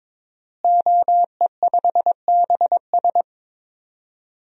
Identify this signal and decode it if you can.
Morse code, heard as a keyed tone, about 22 wpm, 700 Hz. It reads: OE5BS
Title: DXCC-100 Call Signs - 5 Characters - Repeated 3x - 22wpm